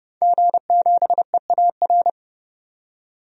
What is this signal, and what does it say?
Morse code, heard as a keyed tone, about 30 wpm, 700 Hz. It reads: G7EAR
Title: DXCC-100 Call Signs - 5 Characters - 30wpm